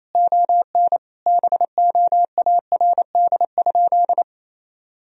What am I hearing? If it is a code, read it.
Morse code, heard as a keyed tone, about 28 wpm, 700 Hz. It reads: ON BOARD?
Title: Sentences from Top 1000 Words - Review Part 4 - 28wpm